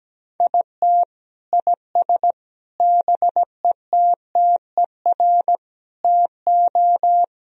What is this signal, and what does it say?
Morse code, heard as a keyed tone, about 17 wpm, 700 Hz. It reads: IT IS BETTER TO
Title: Sentences from Top 1000 Words - Review Part 4 - 17wpm